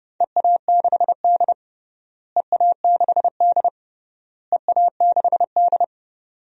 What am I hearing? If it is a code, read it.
Morse code, heard as a keyed tone, about 32 wpm, 700 Hz. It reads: EA6D EA6D EA6D
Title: DXCC-100 Call Signs - 4 Characters - Repeated 3x - 30wpm